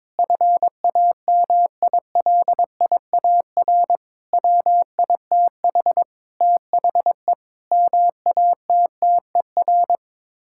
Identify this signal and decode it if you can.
Morse code, heard as a keyed tone, about 22 wpm, 700 Hz. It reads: FAMILIAR WITH THE MATTER
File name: practice / 22wpm / Sets of 7 Words - 22wpm